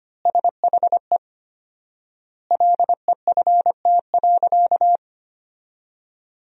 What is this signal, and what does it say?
Morse code, heard as a keyed tone, about 25 wpm, 700 Hz. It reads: SHE LEFT.
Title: 4x Word-Spacing - Sentences from Top 300 Words - 25wpm